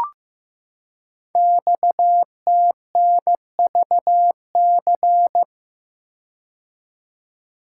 Morse code, heard as a keyed tone, about 15 wpm, 700 Hz. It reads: XTNVC
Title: Code Groups - 5 Letter:Numbers - 15wpm